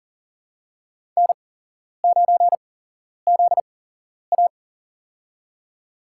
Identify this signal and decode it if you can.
Morse code, heard as a keyed tone, about 40 wpm, spaced as for 11 wpm, 700 Hz. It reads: N9ZA
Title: USA Call Signs - 4 Characters - 8x Character Spacing - 40wpm